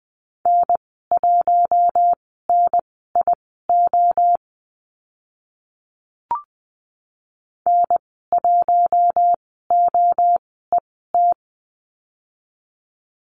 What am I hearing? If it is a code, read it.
Morse code, heard as a keyed tone, about 20 wpm, spaced as for 14 wpm, 700 Hz. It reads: N1NIO N1OET
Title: 042 Lesson 8g - Number 1 - Call Signs 20wpm